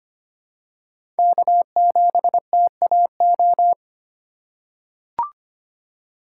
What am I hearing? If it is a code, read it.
Morse code, heard as a keyed tone, about 25 wpm, 700 Hz. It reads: K7TAO
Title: USA Call Signs - 5 Characters - 2x Character Spacing - 25wpm